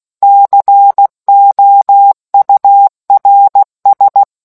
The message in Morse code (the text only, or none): COURS